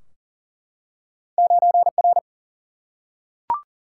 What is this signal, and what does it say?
Morse code, heard as a keyed tone, about 42 wpm, 700 Hz. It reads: 9R
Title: Single-Letter Single-Number 40wpm